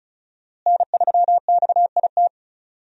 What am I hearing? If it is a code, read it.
Morse code, heard as a keyed tone, about 36 wpm, 700 Hz. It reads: N3XIT
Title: USA Call Signs - 5 Characters - Repeated 3x - 35wpm